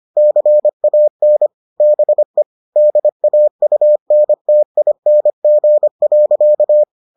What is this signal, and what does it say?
Morse code, heard as a keyed tone, about 25 wpm, 600 Hz. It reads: CAN BE DAUNTING.